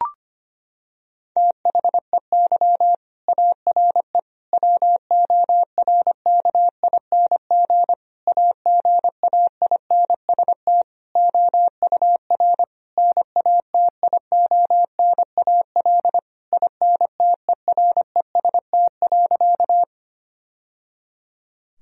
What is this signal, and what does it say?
Morse code, heard as a keyed tone, about 25 wpm, 700 Hz. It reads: THEY ARE WORKING AGAINST OUR NATIONAL INTEREST.
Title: Sentences from Top 1000 Words - Review Part 4 - 25wpm